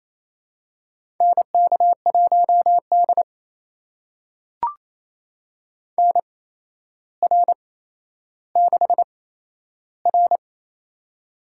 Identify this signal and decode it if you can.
Morse code, heard as a keyed tone, about 28 wpm, spaced as for 12 wpm, 700 Hz. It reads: NK1D NR6R
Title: USA Call Signs - 8x Character Spacing - 28wpm